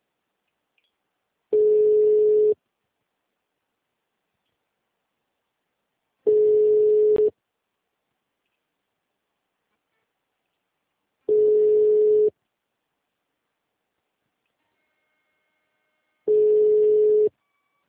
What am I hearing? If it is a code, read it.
Ringback tone (Europe, 425 Hz).
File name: out- -2011- - - .50084